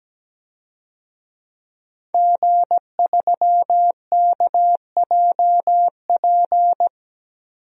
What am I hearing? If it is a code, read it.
Morse code, heard as a keyed tone, about 17 wpm, 700 Hz. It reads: G3KJP
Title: DXCC-100 Call Signs - 5 Characters - 17wpm